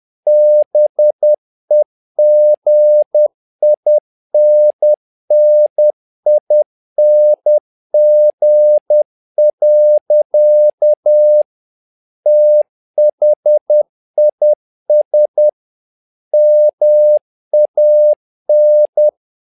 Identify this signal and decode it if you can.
Morse code, heard as a keyed tone, about 10 wpm, 600 Hz. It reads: BEGINNING. THIS MAN
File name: learning center / tower trouble / tower trouble 10wpm